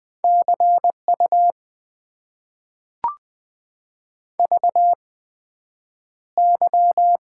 Morse code, heard as a keyed tone, about 20 wpm, spaced as for 9 wpm, 700 Hz. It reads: CU VY
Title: Top 90 QSO Elements - 8x Character Spacing - 20wpm